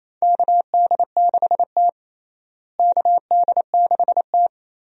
Morse code, heard as a keyed tone, about 28 wpm, 700 Hz. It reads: KD6T KD6T
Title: USA Call Signs - 4 Characters - Repeated 2x - 28wpm